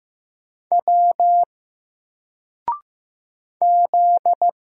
Morse code, heard as a keyed tone, about 15 wpm, 700 Hz. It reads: W Z
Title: Single Letter-Number Alt 15wpm